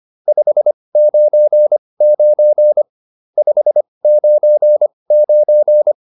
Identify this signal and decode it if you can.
Morse code, heard as a keyed tone, about 25 wpm, spaced as for 20 wpm, 600 Hz. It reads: 599 599